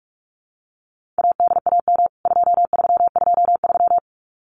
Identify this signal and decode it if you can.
Morse code, heard as a keyed tone, about 44 wpm, 700 Hz. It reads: ADAM 2323
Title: CWT Exchange 45wpm